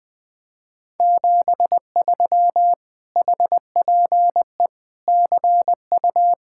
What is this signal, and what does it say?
Morse code, heard as a keyed tone, about 20 wpm, 700 Hz. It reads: 73 HPE CU
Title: Speed-Racing Common QSO Phrases - 20wpm